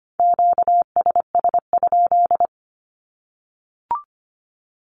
Morse code, heard as a keyed tone, about 25 wpm, 700 Hz. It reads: QSS?
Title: Q-Codes Repeated 3x 25wpm